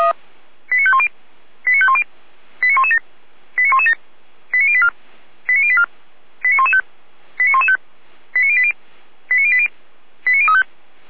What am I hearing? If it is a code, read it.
Five-tone selective call (ZVEI), heard as quick runs of five tones, tones of about 70 ms. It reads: 89510 89510 89117 89117 89084 89084 89115 89115 89080 89080 89035